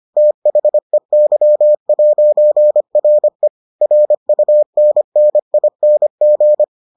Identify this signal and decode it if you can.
Morse code, heard as a keyed tone, about 25 wpm, 600 Hz. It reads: THEY'RE RUNNING